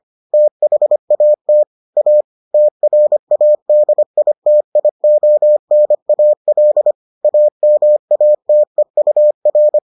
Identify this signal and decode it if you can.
Morse code, heard as a keyed tone, about 25 wpm, 600 Hz. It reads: THAT A TRADITIONAL AMATEUR